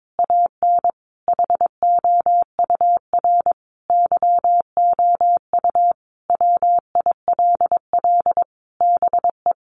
Morse code, heard as a keyed tone, about 22 wpm, 700 Hz. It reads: AN HOUR YOU WILL BE
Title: Sentences from Top 1300 Words 22wpm